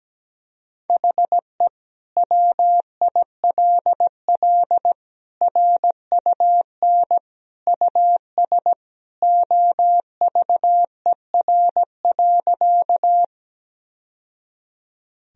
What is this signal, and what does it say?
Morse code, heard as a keyed tone, about 17 wpm, 700 Hz. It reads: HE WILL RUN US OVER.
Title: Sentences from Top 1000 Words - Review Part 1 - 17wpm